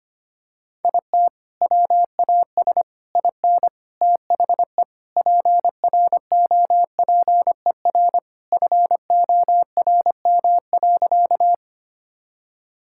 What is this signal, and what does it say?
Morse code, heard as a keyed tone, about 25 wpm, 700 Hz. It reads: IT WAS IN THE PROPER FORM.
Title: Sentences from Top 1000 Words - Review Part 5 - 25wpm